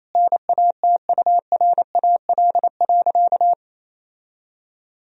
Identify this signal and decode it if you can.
Morse code, heard as a keyed tone, about 28 wpm, 700 Hz. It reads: NATURAL.
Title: Sentences from Top 1000 Words - Review Part 4 - 28wpm